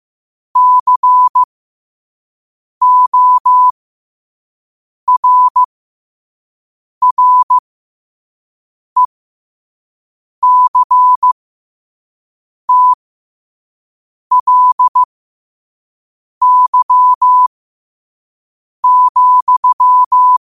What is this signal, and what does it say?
Morse code, heard as a keyed tone, about 15 wpm, spaced as for 5 wpm, 1000 Hz. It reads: CORRECTLY,
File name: qotdfiles / 5wpm